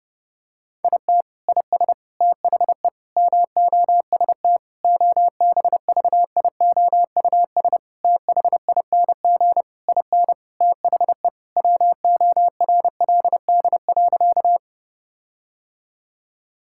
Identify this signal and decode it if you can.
Morse code, heard as a keyed tone, about 30 wpm, 700 Hz. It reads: IT IS THE MOST OBVIOUS THING IN THE WORLD.
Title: Sentences from Top 1600 Words 30wpm